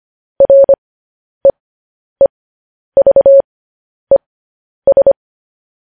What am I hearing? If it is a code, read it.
Morse code, heard as a keyed tone, about 25 wpm, spaced as for 10 wpm, 550 Hz. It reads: REEVES
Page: news-headlines-in-morse-code-at-10-wpm